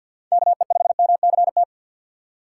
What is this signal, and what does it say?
Morse code, heard as a keyed tone, about 50 wpm, 700 Hz. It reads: KE5MCT
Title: USA Call Signs - 6 Characters - 8x Character Spacing - 50wpm